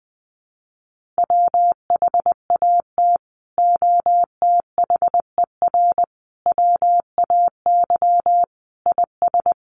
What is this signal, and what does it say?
Morse code, heard as a keyed tone, about 20 wpm, 700 Hz. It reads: WHAT OTHER WAY IS
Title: Sentences from Top 100 Words 20wpm